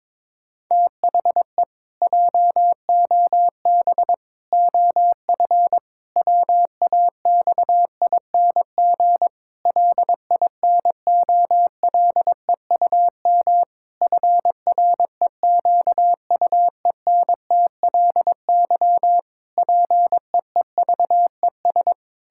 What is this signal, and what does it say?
Morse code, heard as a keyed tone, about 22 wpm, 700 Hz. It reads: THE JOB OF WAXING LINOLEUM FREQUENTLY PEEVES
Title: Pangram Challenge v2 - 22wpm